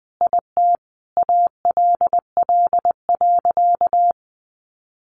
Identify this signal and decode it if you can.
Morse code, heard as a keyed tone, about 20 wpm, 700 Hz. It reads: IT ALL.
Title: Sentences from Top 200 Words 20wpm